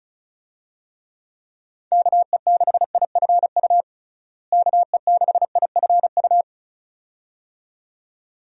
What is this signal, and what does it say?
Morse code, heard as a keyed tone, about 36 wpm, 700 Hz. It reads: KE6IFU KE6IFU
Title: USA Call Signs - 6 Characters - Repeated 2x - 35wpm